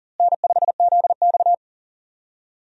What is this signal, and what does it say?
Morse code, heard as a keyed tone, about 39 wpm, 700 Hz. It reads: N5ZX
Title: USA Call Signs - 4 Characters - Repeated 3x - 40wpm